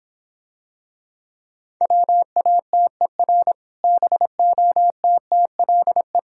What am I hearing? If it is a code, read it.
Morse code, heard as a keyed tone, about 26 wpm, 700 Hz. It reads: WATER BOTTLE